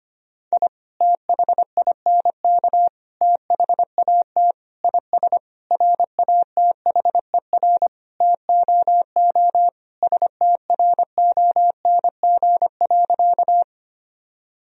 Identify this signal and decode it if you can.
Morse code, heard as a keyed tone, about 25 wpm, 700 Hz. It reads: I THINK THAT IS RATHER TOO STRONG.
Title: Sentences from Top 1000 Words - Review Part 4 - 25wpm